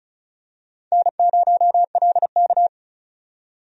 Morse code, heard as a keyed tone, about 36 wpm, 700 Hz. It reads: N0LK
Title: USA Call Signs - 4 Characters - 35wpm